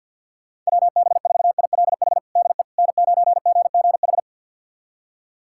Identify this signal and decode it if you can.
Morse code, heard as a keyed tone, about 48 wpm, 700 Hz. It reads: WB4ILH DE N9GGH
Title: USA Callsign DE Callsign 50wpm